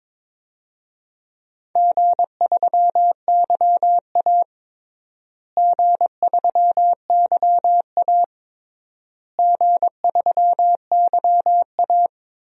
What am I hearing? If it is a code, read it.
Morse code, heard as a keyed tone, about 22 wpm, 700 Hz. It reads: G3YA G3YA G3YA
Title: DXCC-100 Call Signs - 4 Characters - Repeated 3x - 22wpm